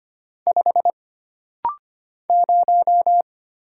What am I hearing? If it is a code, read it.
Morse code, heard as a keyed tone, about 25 wpm, 700 Hz. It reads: 5 0
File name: Single Character (Letter:Number:Punctuation:Prosign) - Rapid-Fire - 25wpm